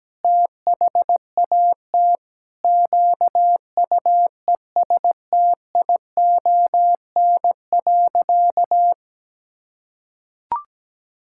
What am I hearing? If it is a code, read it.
Morse code, heard as a keyed tone, about 17 wpm, 700 Hz. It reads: THAT QUESTION.
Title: Sentences from Top 1000 Words - Review Part 5 - 17wpm